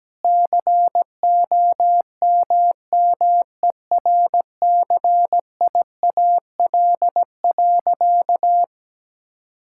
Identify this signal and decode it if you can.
Morse code, heard as a keyed tone, about 17 wpm, 700 Hz. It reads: COMMERCIAL.
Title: Sentences from Top 1000 Words - Review Part 5 - 17wpm